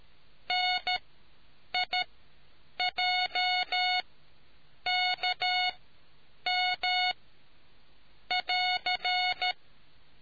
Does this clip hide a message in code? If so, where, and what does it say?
Yes: Morse code from 0.5 to 9.5 s, heard as a keyed tone, about 13 wpm, spaced as for 7 wpm, 730 Hz. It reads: NIJKM<AR>